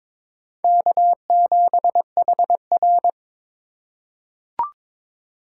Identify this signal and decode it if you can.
Morse code, heard as a keyed tone, about 22 wpm, 700 Hz. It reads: K7HR
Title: USA Call Signs - 4 Characters - 6x Character Spacing - 22wpm